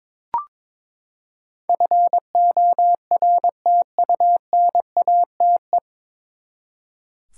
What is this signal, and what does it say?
Morse code, heard as a keyed tone, about 22 wpm, 700 Hz. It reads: FORTUNATE
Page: Morse Code Ninja practice 22wpm